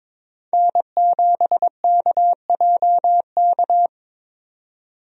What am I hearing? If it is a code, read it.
Morse code, heard as a keyed tone, about 22 wpm, 700 Hz. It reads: N7KJK
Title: USA Call Signs - 5 Characters - 22wpm